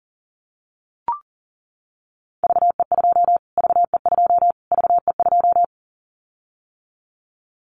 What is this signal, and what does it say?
Morse code, heard as a keyed tone, about 40 wpm, 700 Hz. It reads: VE2 VE2 VE2
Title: Callsign Prefixes for Top 25 DXCC - Repeated 3x - 40wpm